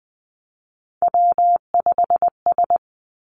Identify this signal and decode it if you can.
Morse code, heard as a keyed tone, about 20 wpm, 700 Hz. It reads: W5S
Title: Three Challenging Letter-Number Combinations 20wpm